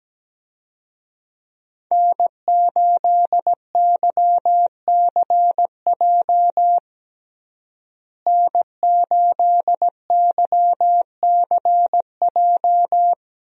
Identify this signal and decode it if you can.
Morse code, heard as a keyed tone, about 17 wpm, 700 Hz. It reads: N8YCJ N8YCJ
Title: USA Call Signs - 5 Characters - Repeated 3x - 17wpm